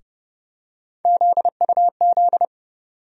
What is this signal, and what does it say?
Morse code, heard as a keyed tone, about 30 wpm, 700 Hz. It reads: ZUZ
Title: USA Call Sign Postfixes - 3 Letters 30wpm